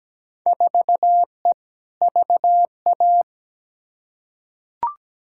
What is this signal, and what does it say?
Morse code, heard as a keyed tone, about 17 wpm, 700 Hz. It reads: 4E VA
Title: Field Day Exchanges Encore - 17wpm